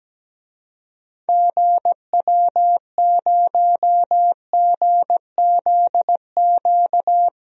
Morse code, heard as a keyed tone, about 17 wpm, 700 Hz. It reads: GW0GZQ
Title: DXCC-100 Call Signs - 6 Characters - 17wpm